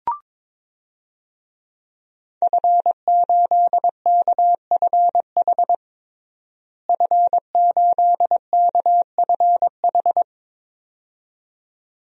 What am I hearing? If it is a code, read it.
Morse code, heard as a keyed tone, about 22 wpm, 700 Hz. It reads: F8KFH F8KFH
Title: DXCC-100 Call Signs - 5 Characters - Repeated 2x - 22wpm